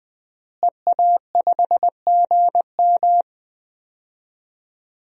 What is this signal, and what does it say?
Morse code, heard as a keyed tone, about 20 wpm, 700 Hz. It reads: EA5GM
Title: DXCC-100 Call Signs - 5 Characters - Repeated 2x - 20wpm